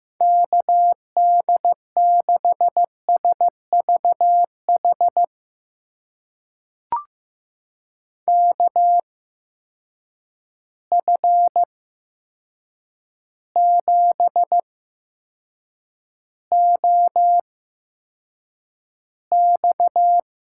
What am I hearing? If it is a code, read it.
Morse code, heard as a keyed tone, about 15 wpm, spaced as for 7 wpm, 700 Hz. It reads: KD6SVH KF7OX
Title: USA Call Signs - 6 Characters - 8x Character Spacing - 15wpm